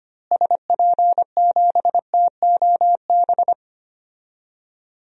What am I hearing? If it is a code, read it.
Morse code, heard as a keyed tone, about 25 wpm, 700 Hz. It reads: SP7TOB
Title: DXCC-100 Call Signs - 6 Characters - Repeated 3x - 25wpm